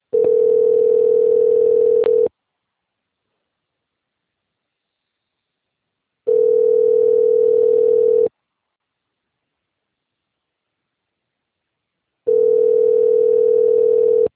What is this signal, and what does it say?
Ringback tone (North America), 440 + 480 Hz.